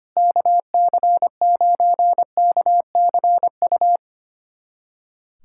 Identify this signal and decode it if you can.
Morse code, heard as a keyed tone, about 25 wpm, 700 Hz. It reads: KC9KCU
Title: USA Callsign DE Callsign 25wpm